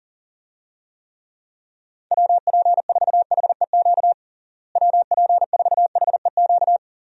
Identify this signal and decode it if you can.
Morse code, heard as a keyed tone, about 40 wpm, 700 Hz. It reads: WP4HEQ WP4HEQ
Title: USA Call Signs - 6 Characters - Repeated 2x - 40wpm